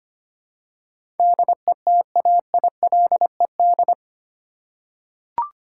Morse code, heard as a keyed tone, about 25 wpm, 700 Hz. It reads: DETAILED
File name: 8 Letter Words - First 3 Letters Given - 25wpm